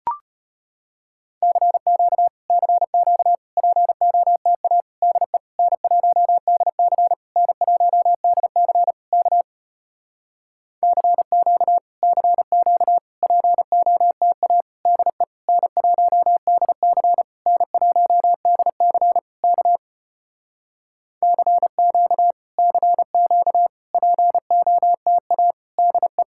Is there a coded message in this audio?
Yes: Morse code from 1.4 to 26.2 s, heard as a keyed tone, about 33 wpm, 700 Hz. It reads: CQ CQ POTA DE N1DC N1DC K CQ CQ POTA DE N1DC N1DC K CQ CQ POTA DE